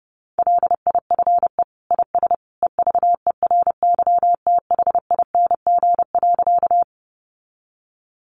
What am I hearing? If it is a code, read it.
Morse code, heard as a keyed tone, about 30 wpm, 700 Hz. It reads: LIFE IS EVERYTHING.
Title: Sentences from Top 700 Words 30wpm